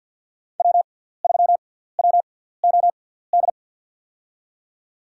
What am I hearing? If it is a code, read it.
Morse code, heard as a keyed tone, about 51 wpm, spaced as for 17 wpm, 700 Hz. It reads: W3WOD
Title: USA Call Signs - 6x Character Spacing - 50wpm